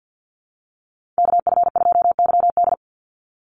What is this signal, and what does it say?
Morse code, heard as a keyed tone, about 50 wpm, 700 Hz. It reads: KF2YD
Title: USA Call Signs 50wpm